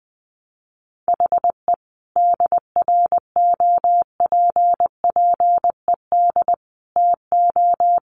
Morse code, heard as a keyed tone, about 20 wpm, 700 Hz. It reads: HE DROPPED TO
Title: A Sound Of Thunder 20wpm